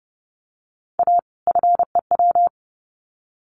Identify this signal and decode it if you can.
Morse code, heard as a keyed tone, about 30 wpm, 700 Hz. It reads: A FEW
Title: Sets of 2 Words 30wpm